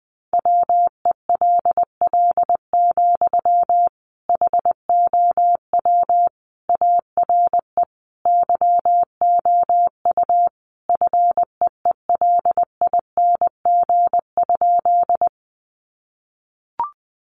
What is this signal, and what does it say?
Morse code, heard as a keyed tone, about 20 wpm, 700 Hz. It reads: WELL, HOW ARE YOU FEELING?
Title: Sentences from Top 1200 Words 20wpm